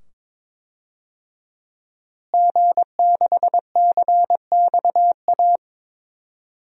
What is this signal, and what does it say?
Morse code, heard as a keyed tone, about 22 wpm, 700 Hz. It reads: G6CXA
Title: DXCC-100 Call Signs - 5 Characters - Repeated 2x - 22wpm